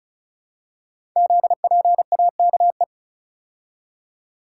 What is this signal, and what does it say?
Morse code, heard as a keyed tone, about 36 wpm, 690 Hz. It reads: ZPAKE
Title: Code Groups - 5 Letter:Numbers - 35wpm